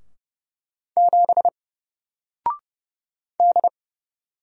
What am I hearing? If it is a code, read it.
Morse code, heard as a keyed tone, about 30 wpm, 700 Hz. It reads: 7 D
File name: Single-Letter-Number-30wpm